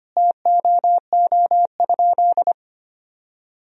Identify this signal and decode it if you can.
Morse code, heard as a keyed tone, about 25 wpm, 700 Hz. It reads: TOO?
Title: Sentences from Top 200 Words and Abbreviations 25wpm